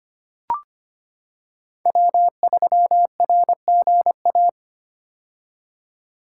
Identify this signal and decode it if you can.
Morse code, heard as a keyed tone, about 25 wpm, 700 Hz. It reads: W3RGA
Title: K1USN SST QSO Practice - 25wpm